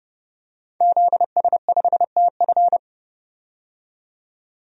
Morse code, heard as a keyed tone, about 30 wpm, 700 Hz. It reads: ZS5TF
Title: DXCC-100 Call Signs - 5 Characters - 30wpm